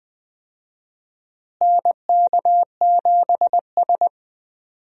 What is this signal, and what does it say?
Morse code, heard as a keyed tone, about 20 wpm, 700 Hz. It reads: NK7S
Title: USA Call Signs - 4 Characters - Repeated 3x - 20wpm